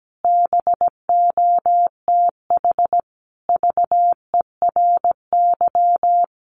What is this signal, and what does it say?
Morse code, heard as a keyed tone, about 17 wpm, 700 Hz. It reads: BOTH VERY
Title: Sets of 5 Words 17wpm